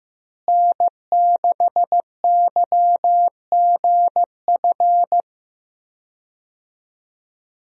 Morse code, heard as a keyed tone, about 15 wpm, 700 Hz. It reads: N6YGF
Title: USA Call Signs - 5 Characters - Repeated 3x - 15wpm